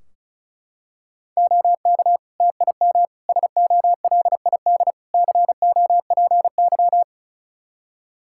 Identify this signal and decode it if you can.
Morse code, heard as a keyed tone, about 34 wpm, 700 Hz. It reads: OK TIM SOLID COPY
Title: Practice QSO - Exchange 3 - More Info - 35wpm